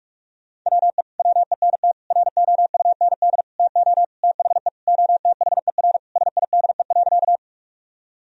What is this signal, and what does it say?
Morse code, heard as a keyed tone, about 46 wpm, 700 Hz. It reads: WE WENT ROUND TO THE OTHER SIDE.
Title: Sentences from Top 1000 Words - Review Part 1 - 45wpm